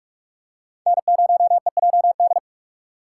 Morse code, heard as a keyed tone, about 40 wpm, 700 Hz. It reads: N0EJD